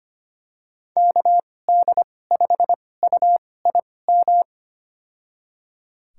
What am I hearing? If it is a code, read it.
Morse code, heard as a keyed tone, about 25 wpm, spaced as for 18 wpm, 700 Hz. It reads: KD5UIM
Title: USA Call Signs - 6 Characters - 2x Character Spacing - 25wpm